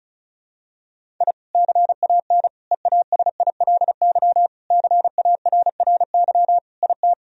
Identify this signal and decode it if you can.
Morse code, heard as a keyed tone, about 36 wpm, 700 Hz. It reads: I CAN EASILY CARRY IT
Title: Sentences from Top 1000 Words - Review Part 1 - 35wpm